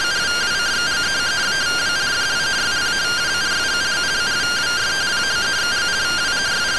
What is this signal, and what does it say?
Two-tone teleprinter signal, about 170 Hz shift.